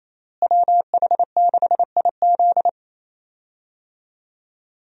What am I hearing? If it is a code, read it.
Morse code, heard as a keyed tone, about 28 wpm, 700 Hz. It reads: WH6IZ
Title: USA Call Signs - 5 Characters - Repeated 2x - 28wpm